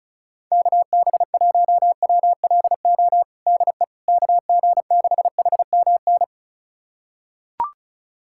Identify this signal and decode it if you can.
Morse code, heard as a keyed tone, about 36 wpm, 700 Hz. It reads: KB1WLO DE KG6HMN
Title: USA Callsign DE Callsign 35wpm